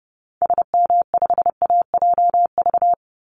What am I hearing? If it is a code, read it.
Morse code, heard as a keyed tone, about 29 wpm, 700 Hz. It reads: SM5AJV
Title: Call Signs Repeated 3x - 30wpm